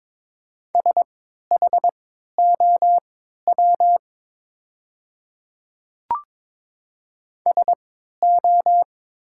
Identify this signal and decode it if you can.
Morse code, heard as a keyed tone, about 22 wpm, spaced as for 12 wpm, 700 Hz. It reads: SHOW SO
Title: 3x Character Spacing - Top 200 Words - 22wpm